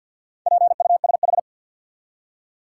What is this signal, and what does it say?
Morse code, heard as a keyed tone, about 53 wpm, 700 Hz. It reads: PUSH
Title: Four Letter Words - 50wpm